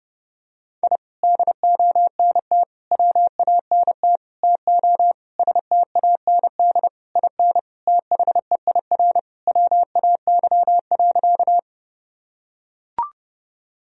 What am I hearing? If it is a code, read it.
Morse code, heard as a keyed tone, about 30 wpm, 700 Hz. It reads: I DONT WANT TO STAND IN THEIR WAY.
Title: Sentences from Top 1000 Words - Review Part 1 - 30wpm